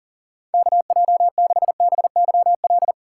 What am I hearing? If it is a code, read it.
Morse code, heard as a keyed tone, about 50 wpm, 700 Hz. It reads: KJ6BYL
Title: Speed Racing - USA Call Signs 40wpm